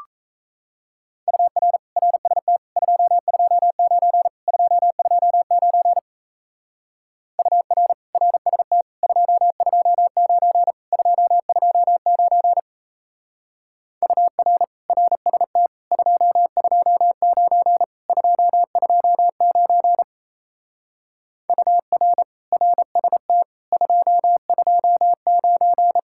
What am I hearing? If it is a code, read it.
Morse code, heard as a keyed tone, about 34 wpm, 700 Hz. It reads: UR RST 229 229 UR RST 229 229 UR RST 229 229 UR RST 229
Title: Speed-Racing Common QSO Phrases - 28wpm